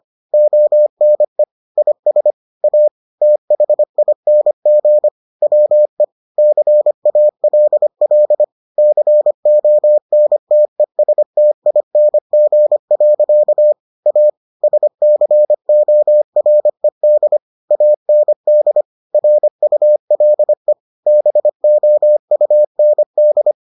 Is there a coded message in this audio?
Yes: Morse code from 0.3 to 23.5 s, heard as a keyed tone, about 25 wpm, 600 Hz. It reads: ONE IS A THING WE CALL CONTESTING. A SCORED AND RULE BOUND